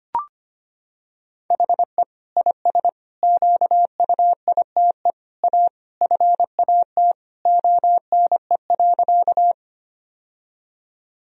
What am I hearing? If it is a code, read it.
Morse code, heard as a keyed tone, about 25 wpm, 700 Hz. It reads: HE IS QUITE A FAT ONE.
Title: Sentences from Top 1000 Words - Review Part 4 - 25wpm